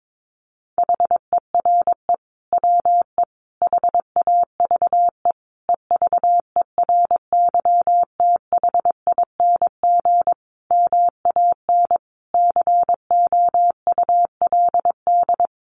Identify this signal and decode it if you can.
Morse code, heard as a keyed tone, about 22 wpm, 700 Hz. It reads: HERE WE HAVE EVERYTHING MAN COULD
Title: Sentences from Top 1100 Words 22wpm